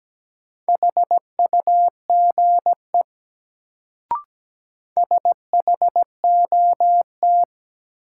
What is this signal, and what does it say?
Morse code, heard as a keyed tone, about 17 wpm, 700 Hz. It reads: HUGE SHOT
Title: Sentences from Top 1100 Words - New Word Drill - 17wpm